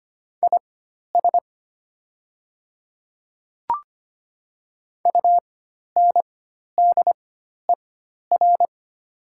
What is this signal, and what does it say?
Morse code, heard as a keyed tone, about 25 wpm, spaced as for 12 wpm, 700 Hz. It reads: IS UNDER
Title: 4x Character Spacing - Top 200 Words - 25wpm